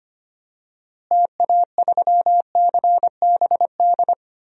Morse code, heard as a keyed tone, about 25 wpm, 700 Hz. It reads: TA3CBD
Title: DXCC-100 Call Signs - 6 Characters - Repeated 3x - 25wpm